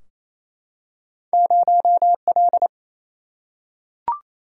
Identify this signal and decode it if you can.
Morse code, heard as a keyed tone, about 28 wpm, 700 Hz. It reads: 0L
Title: Single-Letter Single-Number 28wpm